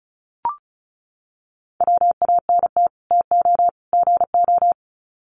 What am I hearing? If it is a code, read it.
Morse code, heard as a keyed tone, about 34 wpm, 700 Hz. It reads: WANT TO GO
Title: Sets of 3 Words 35wpm